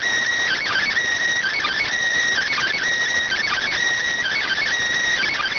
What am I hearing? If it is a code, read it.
Emergency Alert System (SAME) tones, heard as harsh buzzing bursts. It is heard from 0.4 to 4.6 s.